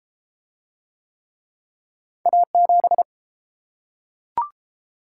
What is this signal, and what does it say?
Morse code, heard as a keyed tone, about 32 wpm, 700 Hz. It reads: A7